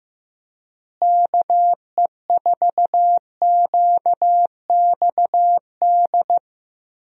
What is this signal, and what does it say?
Morse code, heard as a keyed tone, about 15 wpm, 700 Hz. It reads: KE4QXD
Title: USA Call Signs - 6 Characters - Repeated 3x - 15wpm